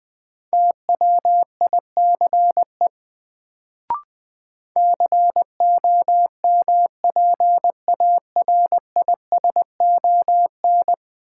Morse code, heard as a keyed tone, about 20 wpm, 700 Hz. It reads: TWICE COMPARISON
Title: Sentences from Top 1600 Words - New Word Drill - 20wpm